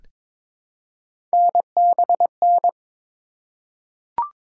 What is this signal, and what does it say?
Morse code, heard as a keyed tone, about 22 wpm, 700 Hz. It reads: NBN
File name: practice / 22wpm / Three Letter-Number Combinations - 22wpm